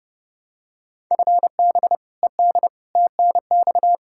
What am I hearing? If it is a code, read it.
Morse code, heard as a keyed tone, about 30 wpm, 700 Hz. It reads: FB ED TNX
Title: Practice QSO - Exchange 3 - More Info - 30wpm